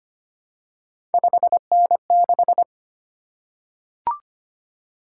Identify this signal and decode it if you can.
Morse code, heard as a keyed tone, about 25 wpm, 700 Hz. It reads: 5N6
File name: Three Letter-Number Combinations - 25wpm